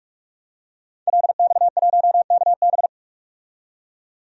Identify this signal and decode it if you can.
Morse code, heard as a keyed tone, about 44 wpm, 700 Hz. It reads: LX1KB
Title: DXCC-100 Call Signs - 5 Characters - 45wpm